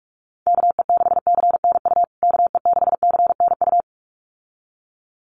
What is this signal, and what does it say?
Morse code, heard as a keyed tone, about 44 wpm, 700 Hz. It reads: KE6CNU KE6CNU
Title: USA Call Signs Repeated 2x 45wpm